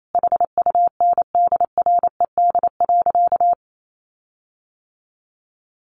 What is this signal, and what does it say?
Morse code, heard as a keyed tone, about 28 wpm, 700 Hz. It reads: HUNDRED.
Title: Sentences from Top 1300 Words 28wpm